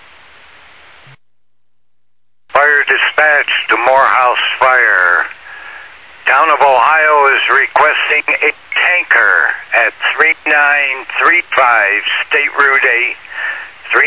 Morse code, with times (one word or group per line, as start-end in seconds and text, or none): none